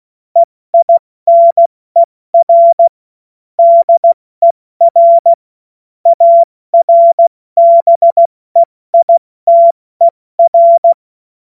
Morse code, heard as a keyed tone, about 16 wpm, 680 Hz. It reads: EINER DER ARBEITER
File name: Morsekurs / Witze / Witz 22. August 2023 70 F80